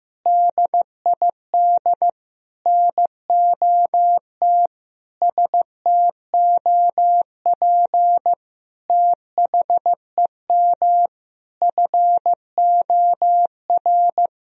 Morse code, heard as a keyed tone, about 15 wpm, 700 Hz. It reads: DID NOT STOP THEM FOR
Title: Sentences from Top 1000 Words - Review Part 1 - 15wpm